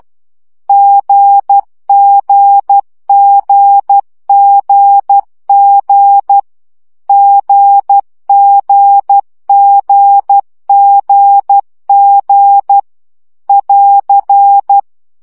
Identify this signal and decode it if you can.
Morse code, heard as a keyed tone, about 12 wpm, 800 Hz. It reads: GGGGG GGGGG <AR>
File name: CW gyakorlas g 10 baud